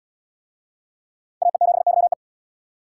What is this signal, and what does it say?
Morse code, heard as a keyed tone, about 70 wpm, 700 Hz. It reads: AE7YE